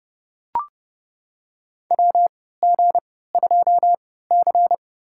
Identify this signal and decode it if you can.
Morse code, heard as a keyed tone, about 30 wpm, spaced as for 17 wpm, 700 Hz. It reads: WG2C
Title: USA Call Signs - 4 Characters - 3x Character Spacing - 30wpm